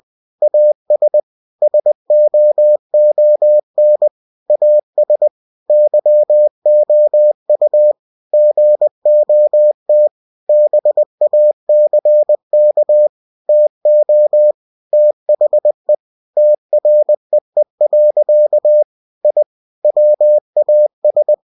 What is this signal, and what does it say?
Morse code, heard as a keyed tone, about 20 wpm, 600 Hz. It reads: AS SOON AS YOU GOT BACK TO THE TREE. I WAS